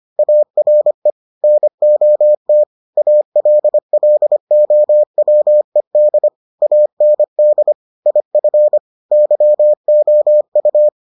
Morse code, heard as a keyed tone, about 25 wpm, 600 Hz. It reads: ARE NOT ALLOWED AND IF YOU